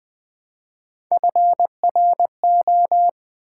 Morse code, heard as a keyed tone, about 20 wpm, 700 Hz. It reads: FRO
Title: Build A Word - 1 Letter At A Time - 4-Letter Words - 20wpm